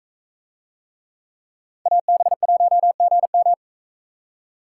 Morse code, heard as a keyed tone, about 41 wpm, 700 Hz. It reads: AB1GM